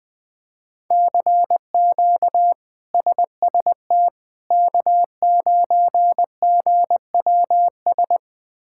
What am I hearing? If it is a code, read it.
Morse code, heard as a keyed tone, about 20 wpm, 700 Hz. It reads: CQ SST K9GWS
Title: K1USN SST QSO Practice - 20wpm